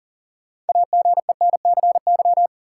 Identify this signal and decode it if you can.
Morse code, heard as a keyed tone, about 38 wpm, 700 Hz. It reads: AGENCY
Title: Sentences from Top 1300 Words - New Word Drill Alt - 40wpm